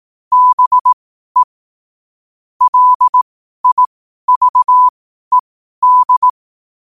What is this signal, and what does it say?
Morse code, heard as a keyed tone, about 18 wpm, spaced as for 12 wpm, 1000 Hz. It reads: BE LIVED
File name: qotdfiles / 12wpm